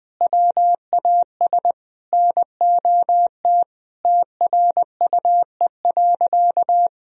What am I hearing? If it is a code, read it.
Morse code, heard as a keyed tone, about 21 wpm, 700 Hz. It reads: WAS NOT TRUE.
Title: Sentences from Top 1000 Words - Review Part 4 - 20wpm